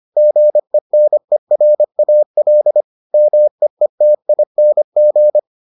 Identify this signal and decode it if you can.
Morse code, heard as a keyed tone, about 25 wpm, 600 Hz. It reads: GENERAL MEETING